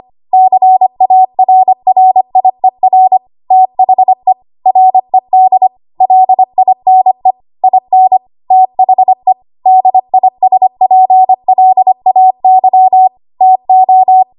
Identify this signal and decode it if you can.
Morse code, heard as a keyed tone, about 25 wpm, 750 Hz. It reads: CARRIER THE RED LINE IN THE DISPLAY TO
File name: Archive / 25 WPM / 25WPM